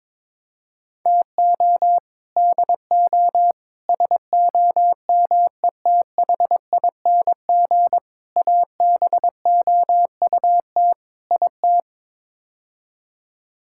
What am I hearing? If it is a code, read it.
Morse code, heard as a keyed tone, about 22 wpm, 700 Hz. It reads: TO DO SOMETHING ABOUT IT
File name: practice / 22wpm / Build A Phrase - 5-Word Phrases - 22wpm